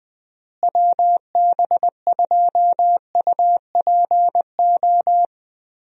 Morse code, heard as a keyed tone, about 20 wpm, 700 Hz. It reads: WB2UPO
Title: USA Call Signs - 6 Characters - Repeated 3x - 20wpm